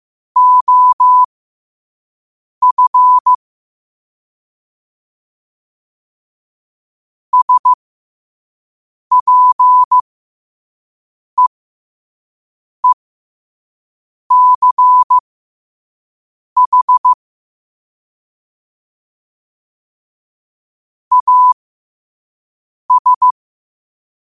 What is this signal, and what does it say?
Morse code, heard as a keyed tone, about 15 wpm, spaced as for 5 wpm, 1000 Hz. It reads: OF SPEECH AS
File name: qotdfiles / 5wpm